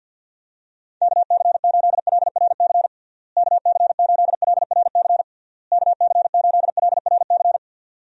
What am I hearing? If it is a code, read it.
Morse code, heard as a keyed tone, about 51 wpm, 700 Hz. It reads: KC8LRC KC8LRC KC8LRC
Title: USA Call Signs - 6 Characters - Repeated 3x - 50wpm